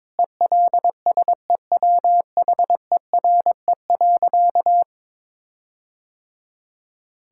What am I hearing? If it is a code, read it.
Morse code, heard as a keyed tone, about 22 wpm, 700 Hz. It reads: ELSEWHERE.
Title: Sentences from Top 1600 Words 22wpm